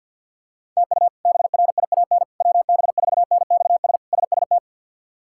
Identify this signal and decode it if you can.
Morse code, heard as a keyed tone, about 46 wpm, 700 Hz. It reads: TU BRIAN WB4NXS SST
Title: K1USN SST QSO Practice - 50wpm